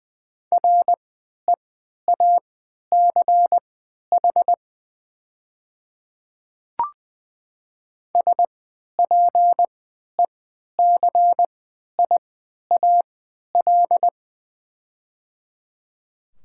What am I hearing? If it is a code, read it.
Morse code, heard as a keyed tone, about 21 wpm, spaced as for 12 wpm, 700 Hz. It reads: REACH SPECIAL
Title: 3x Character Spacing - Top 500 Words - 20wpm